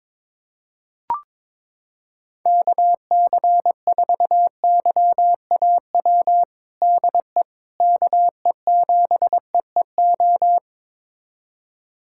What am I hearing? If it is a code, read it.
Morse code, heard as a keyed tone, about 22 wpm, 700 Hz. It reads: KC4YAW DE KE7EEO
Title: USA Callsign DE Callsign 22wpm